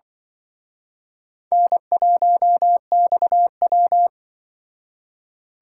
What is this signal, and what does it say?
Morse code, heard as a keyed tone, about 24 wpm, 700 Hz. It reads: N1XW
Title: Speed Racing - USA Call Signs 20wpm